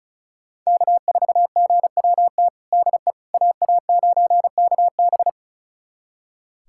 Morse code, heard as a keyed tone, about 34 wpm, 700 Hz. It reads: K4GWT DE AA9KB
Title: USA Callsign DE Callsign 35wpm